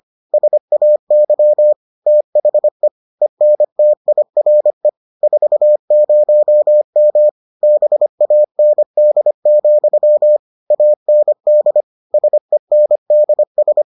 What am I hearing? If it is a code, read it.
Morse code, heard as a keyed tone, about 25 wpm, 600 Hz. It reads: SAY THE ENTIRE 40M BAND, AND SENDS